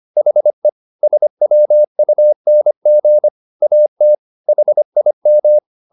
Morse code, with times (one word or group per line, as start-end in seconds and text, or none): 0.2-0.7 HE
1.0-3.3 SWUNG
3.6-4.2 AT
4.5-5.6 HIM